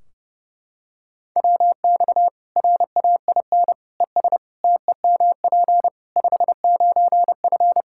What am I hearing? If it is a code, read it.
Morse code, heard as a keyed tone, about 32 wpm, 700 Hz. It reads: WX RAIN ES TEMP 59F
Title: Practice QSO - Exchange 2 - Info - 30wpm